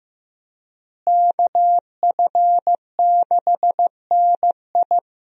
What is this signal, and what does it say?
Morse code, heard as a keyed tone, about 15 wpm, 700 Hz. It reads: KF6NI
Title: USA Call Signs - 5 Characters - Repeated 3x - 15wpm